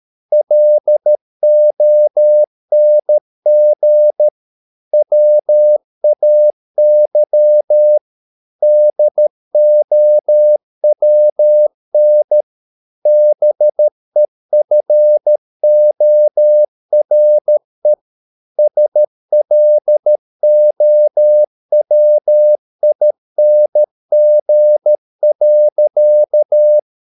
Morse code, heard as a keyed tone, about 13 wpm, 600 Hz. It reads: LONG WAY DOWN BEFORE SLOWING.